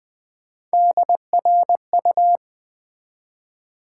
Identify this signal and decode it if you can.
Morse code, heard as a keyed tone, about 20 wpm, 700 Hz. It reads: DRU
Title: Build A Word - 1 Letter At A Time - 4-Letter Words - 20wpm